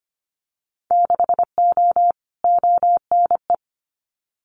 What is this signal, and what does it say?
Morse code, heard as a keyed tone, about 25 wpm, 700 Hz. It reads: 6O ONE
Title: Winter Field Day Exchanges Encore 25wpm